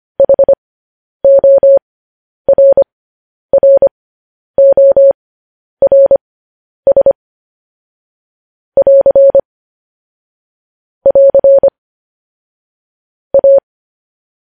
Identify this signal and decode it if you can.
Morse code, heard as a keyed tone, about 25 wpm, spaced as for 10 wpm, 550 Hz. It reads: HORRORS <AR> <AR> A
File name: podcast / News headlines-1x-25x10